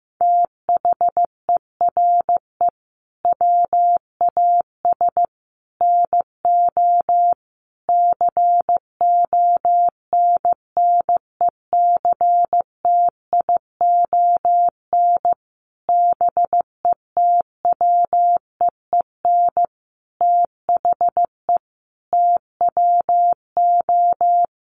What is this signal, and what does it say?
Morse code, heard as a keyed tone, about 14 wpm, 700 Hz. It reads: THERE WAS NO CONNECTION BETWEEN THE TWO
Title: Sentences from Top 1300 Words 15wpm